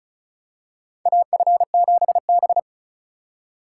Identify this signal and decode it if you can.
Morse code, heard as a keyed tone, about 36 wpm, 700 Hz. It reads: AF7B
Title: USA Call Signs - 4 Characters - 2x Character Spacing - 35wpm